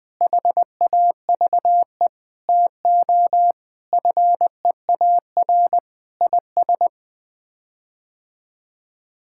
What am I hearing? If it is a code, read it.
Morse code, heard as a keyed tone, about 21 wpm, 700 Hz. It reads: HAVE TO FEAR IS
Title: Sets of 8 Words - 20wpm